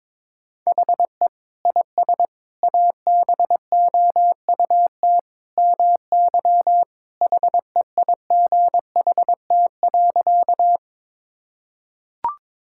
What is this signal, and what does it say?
Morse code, heard as a keyed tone, about 22 wpm, 700 Hz. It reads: HE IS ABOUT MY HEIGHT.
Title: Sentences from Top 1600 Words 22wpm